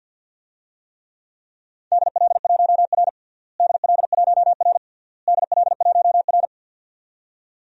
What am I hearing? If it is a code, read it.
Morse code, heard as a keyed tone, about 51 wpm, 700 Hz. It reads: DL1R DL1R DL1R
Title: DXCC-100 Call Signs - 4 Characters - Repeated 3x - 50wpm